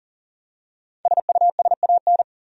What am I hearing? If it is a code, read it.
Morse code, heard as a keyed tone, about 40 wpm, 700 Hz. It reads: SUSAN
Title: USA Common Names - Max 5 Letters - 40wpm